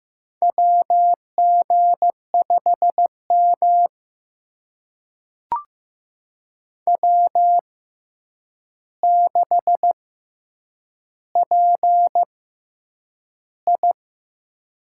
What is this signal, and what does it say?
Morse code, heard as a keyed tone, about 15 wpm, spaced as for 8 wpm, 700 Hz. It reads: WG5M W6PI
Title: USA Call Signs - 4 Characters - 6x Character Spacing - 15wpm